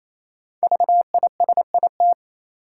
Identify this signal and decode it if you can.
Morse code, heard as a keyed tone, about 28 wpm, 700 Hz. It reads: VISIT
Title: Sentences from Top 900 Words - New Word Drill - 28wpm